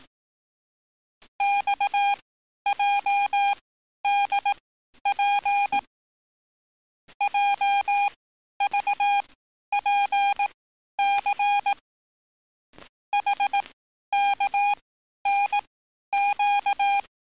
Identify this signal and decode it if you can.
Morse code, heard as a keyed tone, about 18 wpm, spaced as for 11 wpm, 810 Hz. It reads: XJDP JVPC HKNQ
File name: training / NAVPERS 11049to11060 / NAVPERS 11051 RA RB